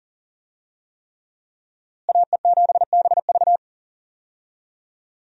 Morse code, heard as a keyed tone, about 40 wpm, 700 Hz. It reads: AE7BV
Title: USA Call Signs - 5 Characters - Repeated 3x - 40wpm